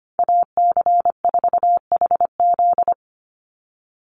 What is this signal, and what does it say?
Morse code, heard as a keyed tone, about 25 wpm, 700 Hz. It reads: AC4HZ
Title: USA Call Signs 25wpm